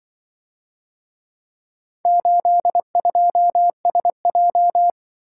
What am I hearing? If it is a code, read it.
Morse code, heard as a keyed tone, about 23 wpm, 700 Hz. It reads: 82SJ
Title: Speed-Racing - Four Letter-Number Combinations - 20wpm